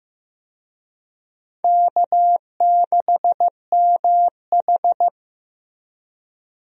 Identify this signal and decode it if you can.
Morse code, heard as a keyed tone, about 15 wpm, 700 Hz. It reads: K6MH
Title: USA Call Signs - 4 Characters - 15wpm